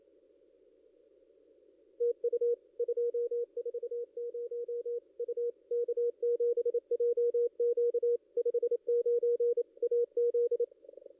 Morse code, heard as a keyed tone, about 27 wpm, 470 Hz. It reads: TU 240 U K7JQ 59 AZ
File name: ss-cw-k5tr-op-n5rz